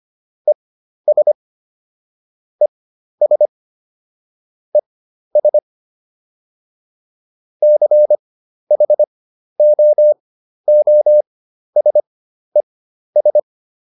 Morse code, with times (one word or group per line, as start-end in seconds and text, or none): 0.5-1.3 ES
2.6-3.5 ES
4.7-5.6 ES
7.6-13.4 CHOOSES